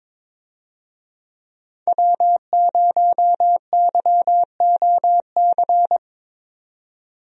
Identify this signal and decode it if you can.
Morse code, heard as a keyed tone, about 22 wpm, 700 Hz. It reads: W0YOC
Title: USA Call Signs - 5 Characters - 22wpm